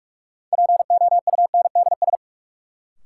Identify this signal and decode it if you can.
Morse code, heard as a keyed tone, about 46 wpm, 700 Hz. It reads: POUNDS
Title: Sentences from Top 1000 Words - Review Part 2 - New Word Drill - 45wpm